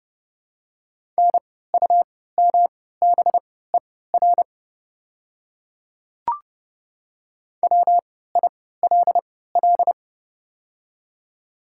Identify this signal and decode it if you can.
Morse code, heard as a keyed tone, about 30 wpm, spaced as for 17 wpm, 690 Hz. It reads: NUMBER WILL
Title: 3x Character Spacing - Top 100 Words - 30wpm